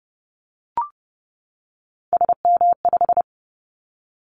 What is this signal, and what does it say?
Morse code, heard as a keyed tone, about 30 wpm, 700 Hz. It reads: SM5
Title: Callsign Prefixes for Top 25 DXCC - 30wpm